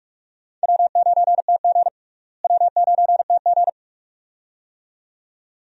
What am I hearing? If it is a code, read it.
Morse code, heard as a keyed tone, about 46 wpm, 700 Hz. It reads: W9TG W9TG
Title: USA Call Signs - 4 Characters - Repeated 2x - 45wpm